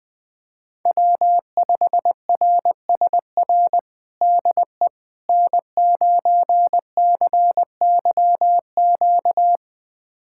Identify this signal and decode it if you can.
Morse code, heard as a keyed tone, about 21 wpm, 700 Hz. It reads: W5RSR DE N9CYQ
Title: USA Callsign DE Callsign - 5 Characters - 20wpm